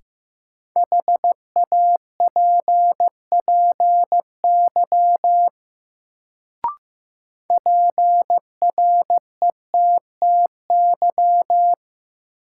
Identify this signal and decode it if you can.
Morse code, heard as a keyed tone, about 15 wpm, 700 Hz. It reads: HAPPY PRETTY
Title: Sentences from Top 1000 Words - New Word Drill - 15wpm